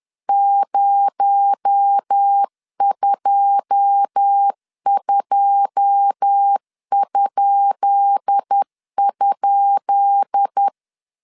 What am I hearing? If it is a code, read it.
Morse code, heard as a keyed tone, about 10 wpm, 800 Hz. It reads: 022??
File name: LEZIONI / 10 SIGNS / 10B INT 10wpm 2844